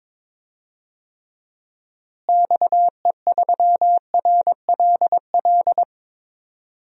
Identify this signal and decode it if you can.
Morse code, heard as a keyed tone, about 21 wpm, 700 Hz. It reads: XE3RLL
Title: DXCC-100 Call Signs - 6 Characters - Repeated 3x - 22wpm